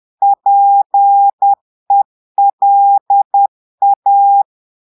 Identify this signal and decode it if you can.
Morse code, heard as a keyed tone, about 10 wpm, 800 Hz. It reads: PELA
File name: Portugais - texte en clair 1 - vitesse - 10 mots minute